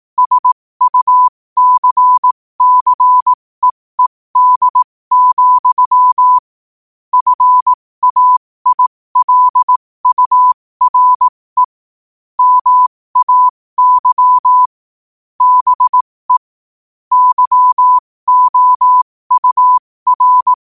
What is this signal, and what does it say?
Morse code, heard as a keyed tone, about 18 wpm, spaced as for 15 wpm, 1000 Hz. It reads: SUCCEED, FAILURE MAY BE YOUR
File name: qotdfiles / 15wpm